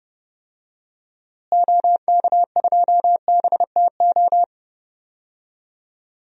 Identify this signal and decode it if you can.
Morse code, heard as a keyed tone, about 30 wpm, 700 Hz. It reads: OK2BTO
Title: DXCC-100 Call Signs - 6 Characters - Repeated 2x - 30wpm